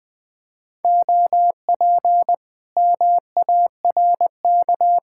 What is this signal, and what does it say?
Morse code, heard as a keyed tone, about 20 wpm, 700 Hz. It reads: OP MARK